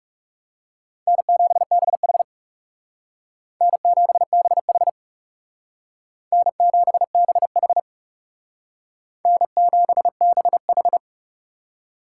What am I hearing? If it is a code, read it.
Morse code, heard as a keyed tone, about 37 wpm, 700 Hz. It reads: N7BH N7BH N7BH N7BH
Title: Speed Racing - USA 1x2 Call Signs 30wpm